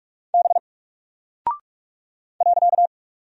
Morse code, heard as a keyed tone, about 44 wpm, 700 Hz. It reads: B .
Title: Single Character (Letter:Number:Punctuation:Prosign) 45wpm